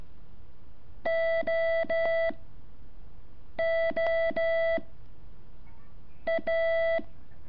Morse code, heard as a keyed tone, about 12 wpm, 660 Hz. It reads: O O A
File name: curso / ejercicio 8